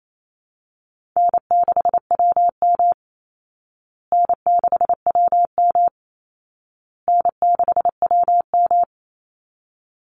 Morse code, heard as a keyed tone, about 28 wpm, 700 Hz. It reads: N6WM N6WM N6WM
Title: Easy Call Signs Repeated 3x - 28wpm